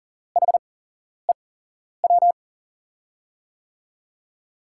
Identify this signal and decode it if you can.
Morse code, heard as a keyed tone, about 40 wpm, spaced as for 11 wpm, 700 Hz. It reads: HEW
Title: USA Call Signs - 6 Characters - 8x Character Spacing - 40wpm